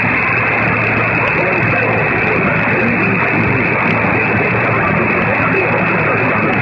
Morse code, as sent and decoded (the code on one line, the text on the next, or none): none